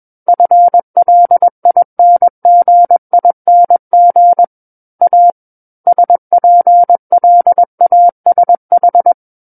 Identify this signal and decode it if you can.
Morse code, heard as a keyed tone, about 21 wpm, 700 Hz. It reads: FLINGING A SPLASH
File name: Book One / 12-Chapter 12 - 21 WPM